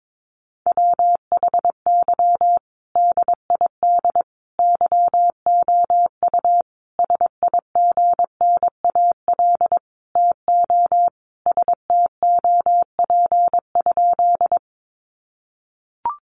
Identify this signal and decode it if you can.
Morse code, heard as a keyed tone, about 22 wpm, 700 Hz. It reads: WHY DID YOU SIGNAL TO STOP?
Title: Sentences from Top 1400 Words 22wpm